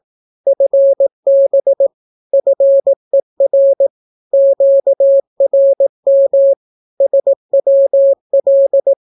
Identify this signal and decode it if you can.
Morse code, heard as a keyed tone, about 18 wpm, 550 Hz. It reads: FB FER QRM SWL
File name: naukapodstaw / skroty / skroty